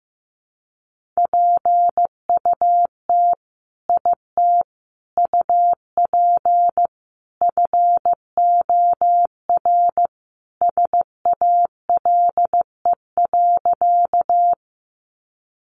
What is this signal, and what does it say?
Morse code, heard as a keyed tone, about 15 wpm, 700 Hz. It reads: PUT IT UP FOR SALE.
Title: Sentences from Top 1400 Words 15wpm